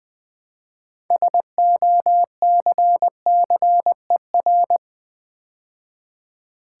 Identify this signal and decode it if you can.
Morse code, heard as a keyed tone, about 20 wpm, 700 Hz. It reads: SOCCER
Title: Build A Word - 3 Letters At A Time - 6-Letter Words - 20wpm